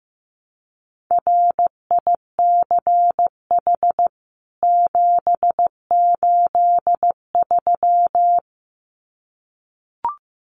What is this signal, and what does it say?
Morse code, heard as a keyed tone, about 15 wpm, 700 Hz. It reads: RICH 783
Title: CWT Exchange 15wpm